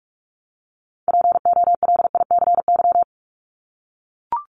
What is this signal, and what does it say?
Morse code, heard as a keyed tone, about 44 wpm, 700 Hz. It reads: POLICY
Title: Top 501-750 Words 45wpm